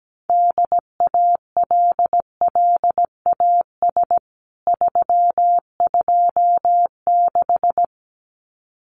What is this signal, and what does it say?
Morse code, heard as a keyed tone, about 17 wpm, 700 Hz. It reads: DALLAS 326
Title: CWT Exchange 17wpm